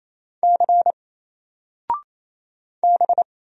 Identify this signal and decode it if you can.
Morse code, heard as a keyed tone, about 27 wpm, 700 Hz. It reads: C B
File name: Single-Letter-Number-28wpm